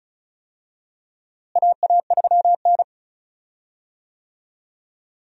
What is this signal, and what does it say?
Morse code, heard as a keyed tone, about 36 wpm, 700 Hz. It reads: AA3N
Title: USA Call Signs - 4 Characters - Repeated 2x - 35wpm